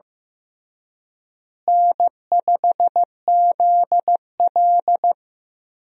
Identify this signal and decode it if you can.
Morse code, heard as a keyed tone, about 15 wpm, 700 Hz. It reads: N5ZL
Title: USA Call Signs - 4 Characters - Repeated 2x - 15wpm